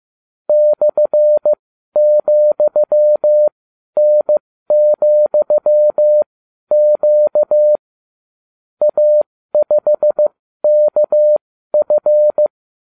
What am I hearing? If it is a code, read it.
Morse code, heard as a keyed tone, about 15 wpm, spaced as for 12 wpm, 600 Hz. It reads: /,N,Q A5KF